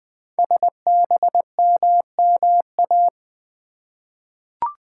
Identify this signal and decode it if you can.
Morse code, heard as a keyed tone, about 20 wpm, 700 Hz. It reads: SBMMA
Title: Code Groups - 5 Letters - 20wpm